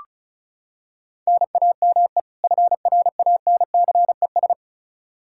Morse code, heard as a keyed tone, about 36 wpm, 700 Hz. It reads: NAME FRANCES
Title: Practice QSO - Exchange 1 - Beginning - 35wpm